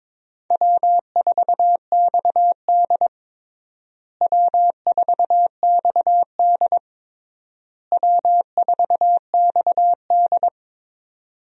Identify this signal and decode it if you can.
Morse code, heard as a keyed tone, about 21 wpm, 700 Hz. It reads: W4XD W4XD W4XD
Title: USA Call Signs - 4 Characters - Repeated 3x - 22wpm